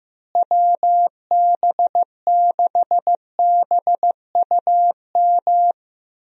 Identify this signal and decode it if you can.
Morse code, heard as a keyed tone, about 15 wpm, 700 Hz. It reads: WB6BUM
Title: USA Call Signs - 6 Characters - Repeated 2x - 15wpm